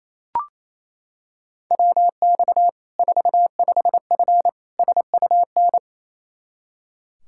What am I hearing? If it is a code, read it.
Morse code, heard as a keyed tone, about 28 wpm, 700 Hz. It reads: WX 45F SUN
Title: K1USN SST QSO Practice 28wpm